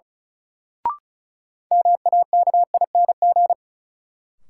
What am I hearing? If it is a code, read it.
Morse code, heard as a keyed tone, about 36 wpm, 700 Hz. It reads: MAKING
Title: Sentences from Top 1000 Words - Review Part 6 - New Word Drill - 35wpm